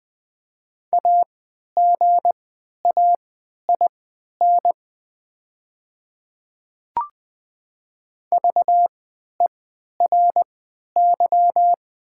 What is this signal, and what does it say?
Morse code, heard as a keyed tone, about 20 wpm, spaced as for 11 wpm, 700 Hz. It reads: AGAIN VERY
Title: 3x Character Spacing - Top 300 Words - 20wpm